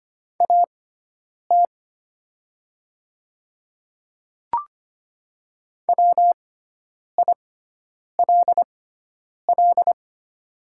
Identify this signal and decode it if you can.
Morse code, heard as a keyed tone, about 25 wpm, spaced as for 9 wpm, 700 Hz. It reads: AT WILL
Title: 6x Character Spacing - Top 200 Words - 25wpm